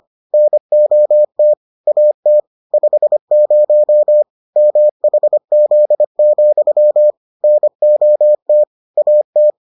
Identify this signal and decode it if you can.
Morse code, heard as a keyed tone, about 25 wpm, 600 Hz. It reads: NOT AT 50 MHZ, NOT AT